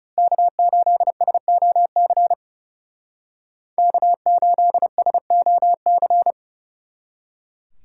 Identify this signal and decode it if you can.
Morse code, heard as a keyed tone, about 39 wpm, 700 Hz. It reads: K8SOC K8SOC